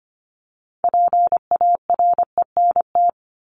Morse code, heard as a keyed tone, about 25 wpm, 700 Hz. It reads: PARENT
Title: Top 1751-2000 Words 25wpm